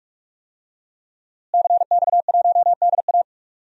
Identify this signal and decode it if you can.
Morse code, heard as a keyed tone, about 45 wpm, 700 Hz. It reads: CX1DA
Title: DXCC-100 Call Signs - 5 Characters - Repeated 3x - 45wpm